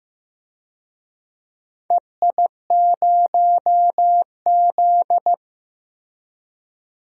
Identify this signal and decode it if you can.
Morse code, heard as a keyed tone, about 15 wpm, 700 Hz. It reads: EI0Z
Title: DXCC-100 Call Signs - 4 Characters - 15wpm